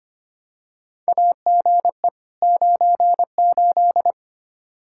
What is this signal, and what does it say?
Morse code, heard as a keyed tone, about 25 wpm, 700 Hz. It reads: AGE 98
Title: Practice QSO - Exchange 3 - More Info - 25wpm